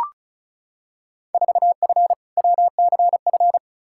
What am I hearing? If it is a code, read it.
Morse code, heard as a keyed tone, about 36 wpm, 700 Hz. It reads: 4F WCF
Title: Field Day Exchanges - 35wpm